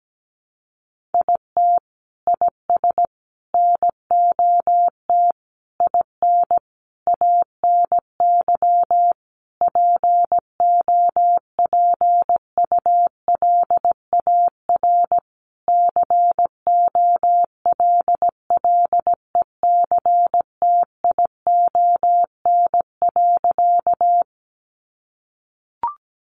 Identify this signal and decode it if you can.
Morse code, heard as a keyed tone, about 17 wpm, 700 Hz. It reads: IT IS NOT IN ANY POPULAR COLLECTION.
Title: Sentences from Top 1200 Words 17wpm